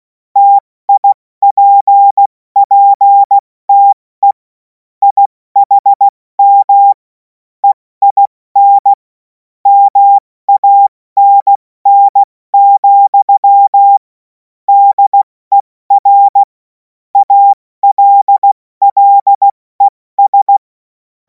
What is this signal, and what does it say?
Morse code, heard as a keyed tone, about 16 wpm, spaced as for 14 wpm, 800 Hz. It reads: TIPPTE IHM EIN MANN, DER ALLES
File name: Morsekurs / Witze / Witz 31. März 2026 70 F80